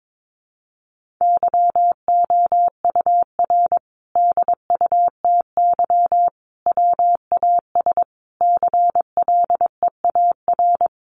Morse code, heard as a keyed tone, about 22 wpm, 700 Hz. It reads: YOUR DUTY WAS CLEAR
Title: Sentences from Top 1500 Words 22wpm